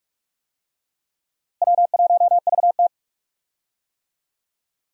Morse code, heard as a keyed tone, about 44 wpm, 700 Hz. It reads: W1VT
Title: USA Call Signs - 4 Characters - Repeated 3x - 45wpm